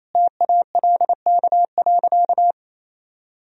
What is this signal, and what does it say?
Morse code, heard as a keyed tone, about 28 wpm, 700 Hz. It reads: TALK.
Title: Sentences from Top 400 Words and Abbreviations 28wpm